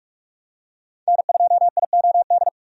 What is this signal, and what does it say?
Morse code, heard as a keyed tone, about 46 wpm, 700 Hz. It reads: N2IOD
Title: USA Call Signs - 5 Characters - 45wpm